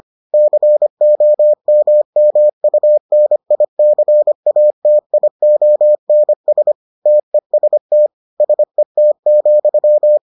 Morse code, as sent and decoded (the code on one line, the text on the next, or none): -.-. --- -- -- ..- -. .. -.-. .- - .. --- -. ... / - . ... - / ... - --..--
COMMUNICATIONS TEST SET,